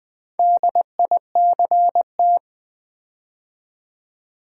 Morse code, heard as a keyed tone, about 20 wpm, 700 Hz. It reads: DICT
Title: Build A Word - 4 Letters At A Time - 8-Letter Words - 20wpm